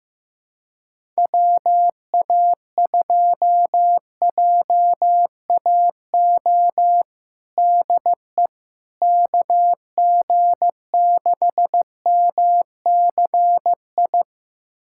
Morse code, heard as a keyed tone, about 15 wpm, 700 Hz. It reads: WA2JAO DE KG6MCI
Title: USA Callsign DE Callsign 15wpm